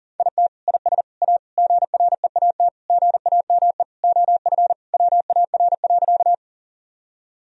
Morse code, heard as a keyed tone, about 40 wpm, 700 Hz. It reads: IT IS A GREAT GAME OF WAR.
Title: Sentences from Top 1000 Words - Review Part 3 - 40wpm